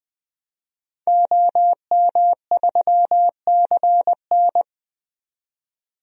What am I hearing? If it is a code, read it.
Morse code, heard as a keyed tone, about 21 wpm, 700 Hz. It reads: OM3CN
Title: DXCC-100 Call Signs - 5 Characters - 20wpm